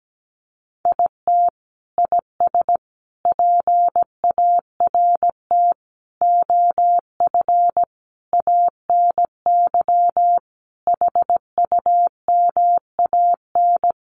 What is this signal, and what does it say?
Morse code, heard as a keyed tone, about 17 wpm, 700 Hz. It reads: IT IS PART OF ANY HUMAN
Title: Sentences from Top 1100 Words 17wpm